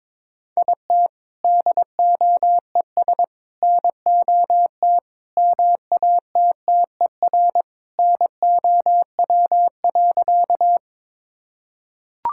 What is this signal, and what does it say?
Morse code, heard as a keyed tone, about 22 wpm, 700 Hz. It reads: IT DOES NOT MATTER NOW.
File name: practice / 22wpm / Sentences with 4-Word Phrases - 22wpm